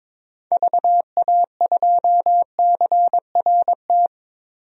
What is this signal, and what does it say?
Morse code, heard as a keyed tone, about 22 wpm, 700 Hz. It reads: VA2CRT
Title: DXCC-100 Call Signs - 6 Characters - 22wpm